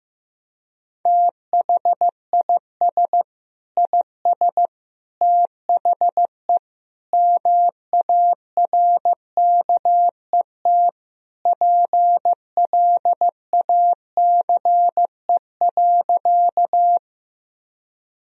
Morse code, heard as a keyed tone, about 15 wpm, 700 Hz. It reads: THIS IS THE MARKET PLACE.
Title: Sentences from Top 1000 Words - Review Part 3 - 15wpm